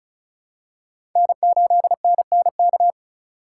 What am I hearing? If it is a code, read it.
Morse code, heard as a keyed tone, about 36 wpm, 700 Hz. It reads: N8NNK
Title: USA Call Signs - 5 Characters - 3x Character Spacing - 35wpm